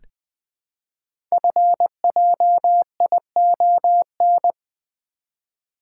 Morse code, heard as a keyed tone, about 20 wpm, 700 Hz. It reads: FJION
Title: Code Groups - 5 Letters - 20wpm